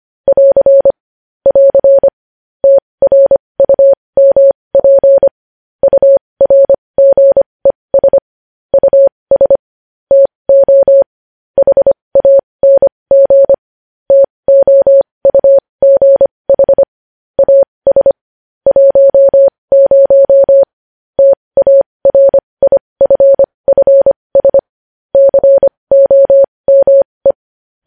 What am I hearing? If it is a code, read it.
Morse code, heard as a keyed tone, about 25 wpm, spaced as for 20 wpm, 550 Hz. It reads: <AR> <AR> TRUMP URGES US TO HANG TOUGH AS 10 TARIFFS COME
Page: News Headlines in Morse Code at 20 WPM